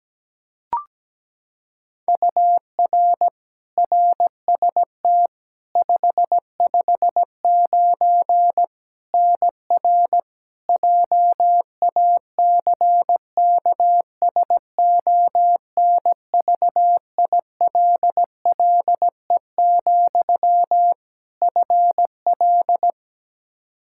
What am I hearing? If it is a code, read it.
Morse code, heard as a keyed tone, about 17 wpm, 700 Hz. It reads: UR RST 559 NR JACKSONVILLE, FL
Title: Practice QSO - Exchange 1 - Beginning - 17wpm